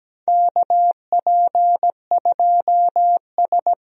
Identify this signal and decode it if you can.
Morse code, heard as a keyed tone, about 17 wpm, 700 Hz. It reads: KP2S
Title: DXCC-100 Call Signs - 4 Characters - Repeated 3x - 17wpm